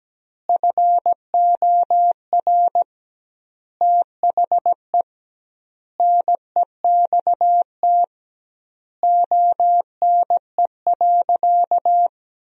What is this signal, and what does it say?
Morse code, heard as a keyed tone, about 17 wpm, 700 Hz. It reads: FOR THE NEXT ONE.
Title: 2x Word-Spacing - Sentences from Top 400 Words - 17wpm